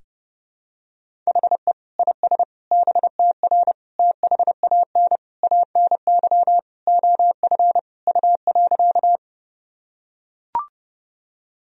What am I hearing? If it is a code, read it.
Morse code, heard as a keyed tone, about 30 wpm, 700 Hz. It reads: HE IS BTR THAN ANY OF U.
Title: Sentences from Top 400 Words and Abbreviations 30wpm